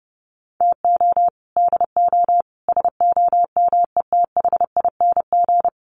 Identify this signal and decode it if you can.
Morse code, heard as a keyed tone, about 30 wpm, 700 Hz. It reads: TO DO SOMETHING
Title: Sets of 5 Words 30wpm